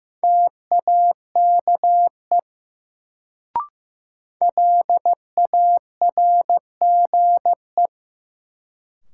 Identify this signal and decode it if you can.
Morse code, heard as a keyed tone, about 15 wpm, 700 Hz. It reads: TAKE LARGE
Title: Sentences from Top 200 Words - New Word Drill - 15wpm